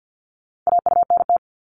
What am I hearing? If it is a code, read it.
Morse code, heard as a keyed tone, about 50 wpm, 700 Hz. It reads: AUNT
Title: Multi-letter Phonemes 50wpm